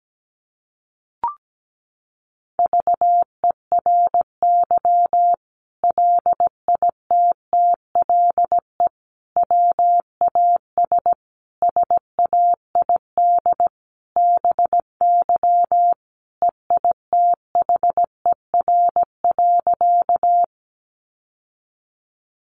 Morse code, heard as a keyed tone, about 17 wpm, 700 Hz. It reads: VERY LITTLE WAS SAID BY EITHER.
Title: Sentences from Top 600 Words 17wpm